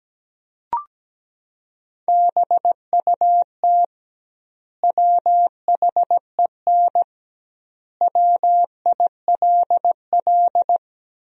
Morse code, heard as a keyed tone, about 17 wpm, 700 Hz. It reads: BUT WHEN WILL
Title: 2x Word-Spacing - Sentences from Top 100 Words - 17wpm